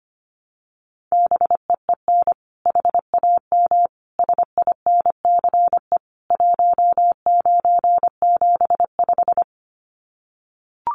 Morse code, heard as a keyed tone, about 25 wpm, 700 Hz. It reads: BEEN HAM SINCE 1975
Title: Common QSO Phrases 25wpm